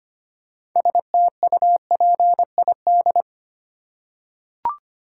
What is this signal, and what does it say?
Morse code, heard as a keyed tone, about 25 wpm, 700 Hz. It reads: STUPID
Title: Six Letter Words - 25wpm